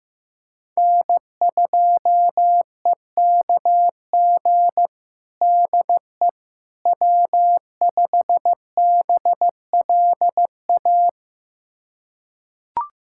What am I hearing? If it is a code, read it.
Morse code, heard as a keyed tone, about 15 wpm, 700 Hz. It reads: N2EKG DE W5BLA
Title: USA Callsign DE Callsign - 5 Characters - 15wpm